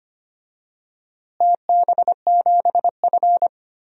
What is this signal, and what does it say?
Morse code, heard as a keyed tone, about 25 wpm, 700 Hz. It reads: TB7F
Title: DXCC-100 Call Signs - 4 Characters - Repeated 2x - 25wpm